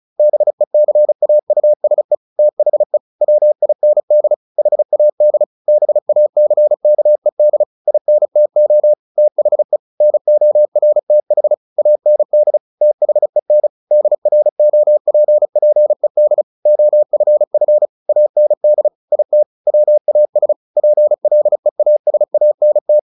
Morse code, heard as a keyed tone, about 34 wpm, 600 Hz. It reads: BECAUSE THE WIND HAD BACKED INTO THE NORTH AND THEN DROPPED OFF AND IT WAS PLEASANT
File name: the old man and the sea / the old man and the sea 1 35wpm